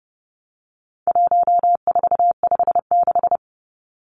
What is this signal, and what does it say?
Morse code, heard as a keyed tone, about 30 wpm, 700 Hz. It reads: 1456
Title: Numbers - 4-Digit 30wpm